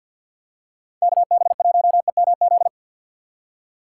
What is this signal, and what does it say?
Morse code, heard as a keyed tone, about 47 wpm, 700 Hz. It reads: KB1EMZ
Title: USA Call Signs - 6 Characters - 4x Character Spacing - 50wpm